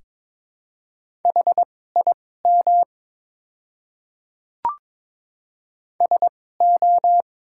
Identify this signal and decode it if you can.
Morse code, heard as a keyed tone, about 22 wpm, spaced as for 16 wpm, 700 Hz. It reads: HIM SO
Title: 2x Character Spacing - Top 100 Words - 22wpm